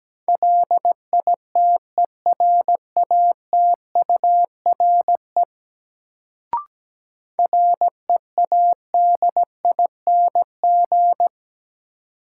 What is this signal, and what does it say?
Morse code, heard as a keyed tone, about 17 wpm, 700 Hz. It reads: LITERATURE READING
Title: Sentences from Top 600 Words - New Word Drill - 17wpm